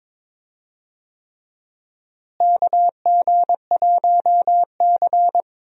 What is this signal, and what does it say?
Morse code, heard as a keyed tone, about 22 wpm, 700 Hz. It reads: KG1C
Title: USA Call Signs - 4 Characters - Repeated 3x - 22wpm